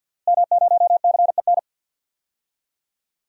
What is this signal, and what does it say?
Morse code, heard as a keyed tone, about 46 wpm, 700 Hz. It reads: M0CEN
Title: DXCC-100 Call Signs - 5 Characters - Repeated 2x - 50wpm